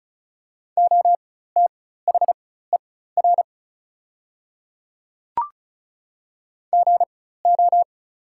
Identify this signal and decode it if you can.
Morse code, heard as a keyed tone, about 36 wpm, spaced as for 16 wpm, 700 Hz. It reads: OTHER GO
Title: 4x Character Spacing - Top 300 Words - 35wpm